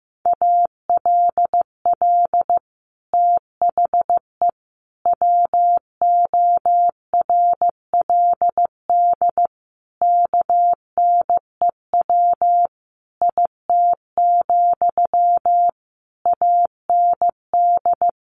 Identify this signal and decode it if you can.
Morse code, heard as a keyed tone, about 15 wpm, 700 Hz. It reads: ALL THE WORLD KNEW IT, AND
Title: Sentences from Top 1200 Words 15wpm